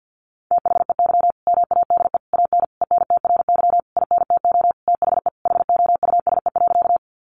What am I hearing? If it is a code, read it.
Morse code, heard as a keyed tone, about 48 wpm, 700 Hz. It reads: THEY MADE AN ENTRY INTO THE HOUSE.
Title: Sentences from Top 1500 Words 50wpm